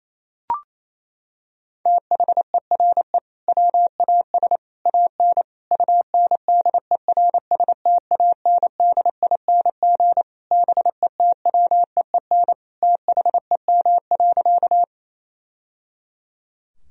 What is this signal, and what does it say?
Morse code, heard as a keyed tone, about 27 wpm, 700 Hz. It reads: THERE WAS AN UNDERSTANDING BETWEEN THEM.
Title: Sentences from Top 1000 Words - Review Part 3 - 28wpm